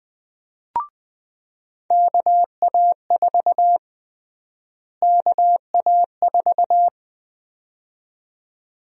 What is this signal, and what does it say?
Morse code, heard as a keyed tone, about 20 wpm, 700 Hz. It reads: KA4 KA4
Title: USA Call Sign Prefixes - Repeated 2x - 20wpm